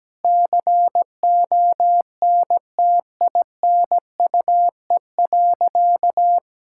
Morse code, heard as a keyed tone, about 17 wpm, 700 Hz. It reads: CONTINUE.
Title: Sentences from Top 1000 Words - Review Part 6 - 17wpm